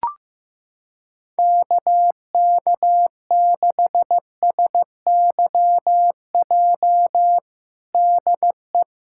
Morse code, heard as a keyed tone, about 15 wpm, 700 Hz. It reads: KK6SYJ DE
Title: USA Callsign DE Callsign - 6 Characters - 15wpm